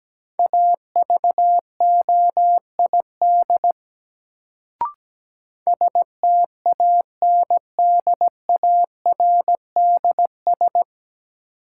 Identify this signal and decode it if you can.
Morse code, heard as a keyed tone, about 17 wpm, 700 Hz. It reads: AVOID STANDARDS
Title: Sentences from Top 1000 Words - Review Part 6 - New Word Drill - 17wpm